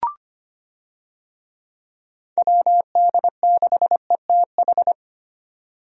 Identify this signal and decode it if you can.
Morse code, heard as a keyed tone, about 25 wpm, 700 Hz. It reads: WD6ETH
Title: USA Call Signs - 6 Characters - 25wpm